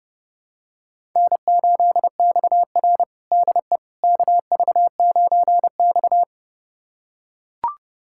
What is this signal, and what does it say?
Morse code, heard as a keyed tone, about 30 wpm, 700 Hz. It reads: N8XR DE KV9X
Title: USA Callsign DE Callsign - 4 Characters - 30wpm